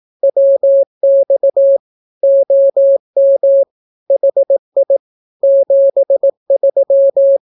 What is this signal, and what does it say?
Morse code, heard as a keyed tone, about 18 wpm, 550 Hz. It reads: WX OM HI 73
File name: naukapodstaw / skroty / skroty